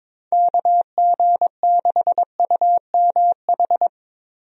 Morse code, heard as a keyed tone, about 22 wpm, 700 Hz. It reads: KG6UMH
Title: USA Call Signs - 6 Characters - Repeated 2x - 22wpm